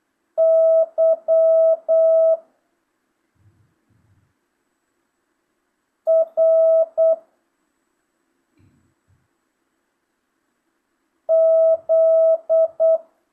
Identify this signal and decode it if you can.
Morse code, heard as a keyed tone, about 8 wpm, 650 Hz. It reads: Y R Z
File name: jh8cbh / 2025cw